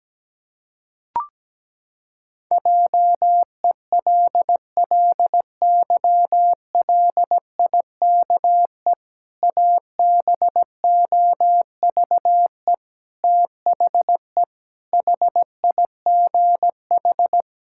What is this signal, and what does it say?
Morse code, heard as a keyed tone, about 17 wpm, 700 Hz. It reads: JELLYLIKE ABOVE THE HIGH
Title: Pangram Challenge v2 - 17wpm